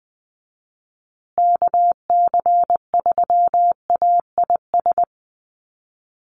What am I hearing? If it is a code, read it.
Morse code, heard as a keyed tone, about 20 wpm, 700 Hz. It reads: KC3AIS
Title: USA Call Signs Repeated 3x 20wpm